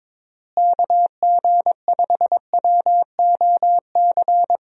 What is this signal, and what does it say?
Morse code, heard as a keyed tone, about 22 wpm, 700 Hz. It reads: KG5WOC
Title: USA Call Signs - 6 Characters - Repeated 3x - 22wpm